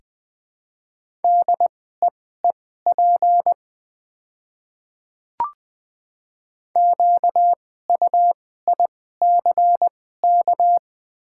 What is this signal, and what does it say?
Morse code, heard as a keyed tone, about 20 wpm, spaced as for 14 wpm, 700 Hz. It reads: DEEP QUICK
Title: 2x Character Spacing - Top 500 Words - 20wpm